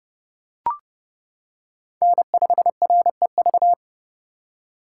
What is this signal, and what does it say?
Morse code, heard as a keyed tone, about 30 wpm, 700 Hz. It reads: N5REV